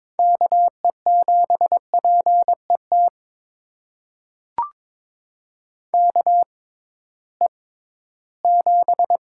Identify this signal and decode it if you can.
Morse code, heard as a keyed tone, about 21 wpm, spaced as for 14 wpm, 700 Hz. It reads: KE7PET KE7
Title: USA Call Signs - 6 Characters - 6x Character Spacing - 22wpm